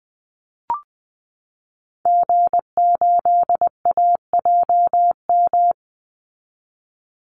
Morse code, heard as a keyed tone, about 20 wpm, 700 Hz. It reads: G8AJM
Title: Call Signs Repeated 2x - 20wpm